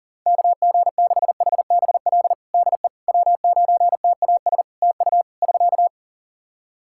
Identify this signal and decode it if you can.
Morse code, heard as a keyed tone, about 37 wpm, 700 Hz. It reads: KG6HBL DE W9TAS TU <SK>
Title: Practice QSO - Exchange 4 - Ending - 40wpm